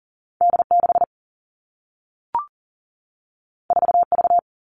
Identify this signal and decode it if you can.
Morse code, heard as a keyed tone, about 40 wpm, 700 Hz. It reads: D6 4V
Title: Two Challenging Letter-Number Combinations 40wpm